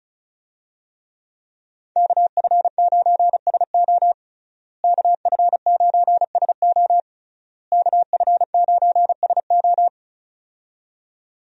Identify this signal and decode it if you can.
Morse code, heard as a keyed tone, about 34 wpm, 700 Hz. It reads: KF9SO KF9SO KF9SO
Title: USA Call Signs - 5 Characters - Repeated 3x - 35wpm